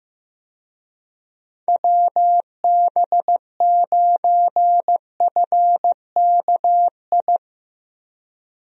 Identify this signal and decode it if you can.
Morse code, heard as a keyed tone, about 15 wpm, 700 Hz. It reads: WB9FKI
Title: USA Call Signs - 6 Characters - Repeated 3x - 15wpm